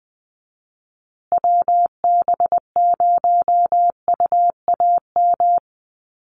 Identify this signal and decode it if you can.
Morse code, heard as a keyed tone, about 20 wpm, 700 Hz. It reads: WB0UAM
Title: USA Call Signs Repeated 2x 20wpm